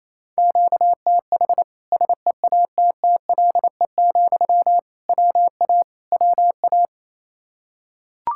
Morse code, heard as a keyed tone, about 28 wpm, 700 Hz. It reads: QTH SEATTLE, WA WA
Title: Practice QSO - Exchange 1 - Beginning - 28wpm